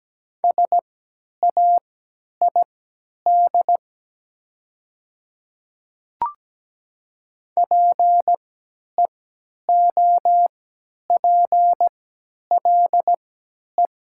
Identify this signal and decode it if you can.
Morse code, heard as a keyed tone, about 17 wpm, spaced as for 10 wpm, 700 Hz. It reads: SAID PEOPLE
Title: 3x Character Spacing - Top 100 Words - 17wpm